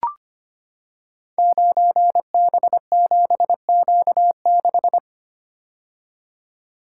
Morse code, heard as a keyed tone, about 25 wpm, 700 Hz. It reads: 9B7Q6
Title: Code Groups - 5 Letter:Numbers - 25wpm